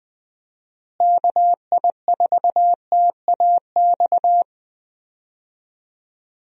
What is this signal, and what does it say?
Morse code, heard as a keyed tone, about 20 wpm, 700 Hz. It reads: KI4TAX
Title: USA Call Signs - 6 Characters - Repeated 3x - 20wpm